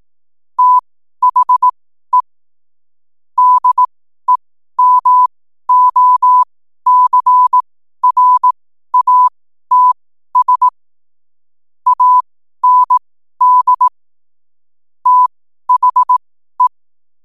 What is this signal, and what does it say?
Morse code, heard as a keyed tone, about 18 wpm, spaced as for 12 wpm, 1000 Hz. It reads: THE DEMOCRATS AND THE
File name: n7net / 12wpm